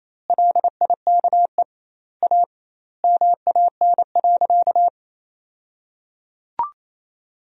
Morse code, heard as a keyed tone, about 27 wpm, 700 Hz. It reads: LIKE A MAN.
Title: 2x Word-Spacing - Sentences from Top 300 Words - 28wpm